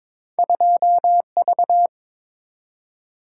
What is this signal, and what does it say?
Morse code, heard as a keyed tone, about 22 wpm, 690 Hz. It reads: 2V
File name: practice / 22wpm / Two Letter-Number Combinations - 22wpm